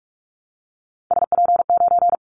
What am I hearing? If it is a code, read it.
Morse code, heard as a keyed tone, about 44 wpm, 700 Hz. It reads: SP9
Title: Callsign Prefixes for Top 25 DXCC - Repeated 3x - 45wpm